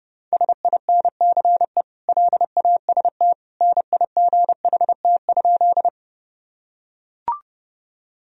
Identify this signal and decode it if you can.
Morse code, heard as a keyed tone, about 30 wpm, 700 Hz. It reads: SINCE LAST NIGHT?
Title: Sentences from Top 1000 Words - Review Part 1 - 30wpm